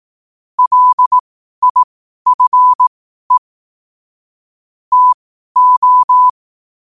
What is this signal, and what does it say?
Morse code, heard as a keyed tone, about 18 wpm, spaced as for 12 wpm, 1000 Hz. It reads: LIFE TO
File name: qotdfiles / 12wpm